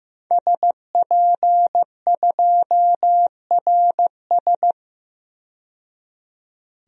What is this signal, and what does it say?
Morse code, heard as a keyed tone, about 16 wpm, 700 Hz. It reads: SP2RS
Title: DXCC-100 Call Signs - 5 Characters - 15wpm